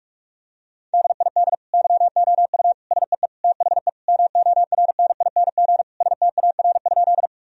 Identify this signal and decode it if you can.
Morse code, heard as a keyed tone, about 46 wpm, 700 Hz. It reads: DID YOU SEE THE MORNING STAR?
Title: Sentences from Top 1000 Words - Review Part 2 - 45wpm